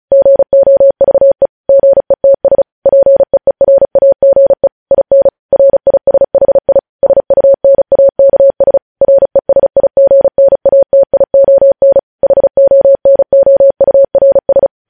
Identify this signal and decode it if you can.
Morse code, heard as a keyed tone, about 36 wpm, 550 Hz. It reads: GOVE GETS PEERAGE IN RISHI SUNAKS RESIGNATION HONOURS
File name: podcast / News headlines-1x-35x35